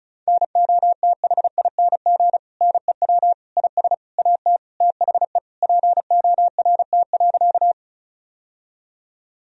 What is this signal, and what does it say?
Morse code, heard as a keyed tone, about 34 wpm, 700 Hz. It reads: NOTHING NEW IS AT THE PORT.
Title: Sentences from Top 1000 Words - Review Part 1 - 35wpm